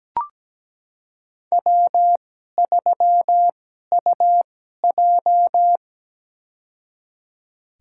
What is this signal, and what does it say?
Morse code, heard as a keyed tone, about 17 wpm, spaced as for 12 wpm, 700 Hz. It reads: W3UJ
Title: USA Call Signs - 4 Characters - 2x Character Spacing - 17wpm